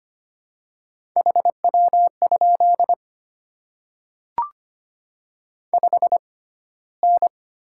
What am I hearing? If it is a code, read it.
Morse code, heard as a keyed tone, about 25 wpm, spaced as for 15 wpm, 700 Hz. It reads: HW? 5N
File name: Top 20 QSO Elements - 6x Character Spacing - 25wpm